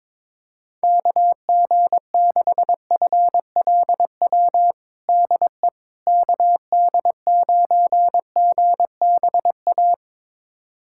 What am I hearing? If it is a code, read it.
Morse code, heard as a keyed tone, about 22 wpm, 700 Hz. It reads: KG6FLW DE KD9GBA
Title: USA Callsign DE Callsign 22wpm